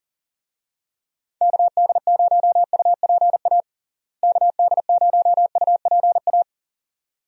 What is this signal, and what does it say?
Morse code, heard as a keyed tone, about 40 wpm, 700 Hz. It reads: KD0UPA KD0UPA
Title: USA Call Signs - 6 Characters - Repeated 2x - 40wpm